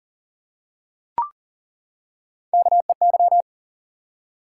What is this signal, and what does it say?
Morse code, heard as a keyed tone, about 39 wpm, 700 Hz. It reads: KEY
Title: Three Letter Words - 40wpm